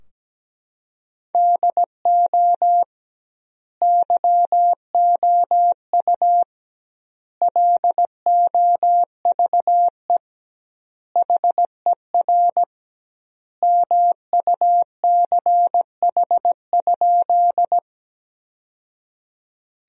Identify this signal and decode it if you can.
Morse code, heard as a keyed tone, about 17 wpm, 700 Hz. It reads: DO YOU LOVE HER MUCH?
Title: 2x Word-Spacing - Sentences from Top 400 Words - 17wpm